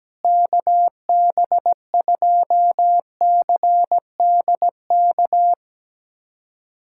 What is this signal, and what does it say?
Morse code, heard as a keyed tone, about 17 wpm, 700 Hz. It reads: KB2CDK
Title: USA Callsign DE Callsign 17wpm